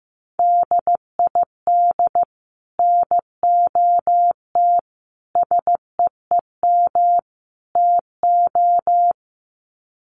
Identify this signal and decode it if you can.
Morse code, heard as a keyed tone, about 15 wpm, 700 Hz. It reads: DID NOT SEEM TO
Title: Sets of 4 Words 15wpm